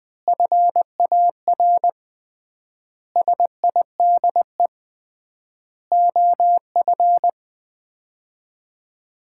Morse code, heard as a keyed tone, about 20 wpm, 700 Hz. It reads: FAR SIDE OF
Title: 3x Word-Spacing - Sets of 5 Words - 20wpm